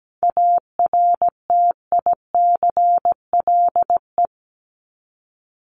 Morse code, heard as a keyed tone, about 17 wpm, 700 Hz. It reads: ARTICLE
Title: Top 751-1000 Words 17wpm